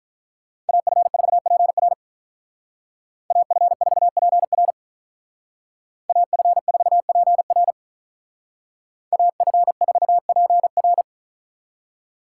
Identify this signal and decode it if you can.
Morse code, heard as a keyed tone, about 43 wpm, 700 Hz. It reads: AF4PR AF4PR AF4PR AF4PR
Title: Speed Racing - USA Call Signs 35wpm